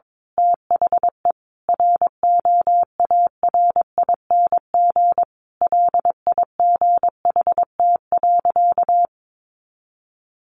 Morse code, heard as a keyed tone, about 22 wpm, 700 Hz. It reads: THE ROARING LIGHT.
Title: A Sound Of Thunder 22wpm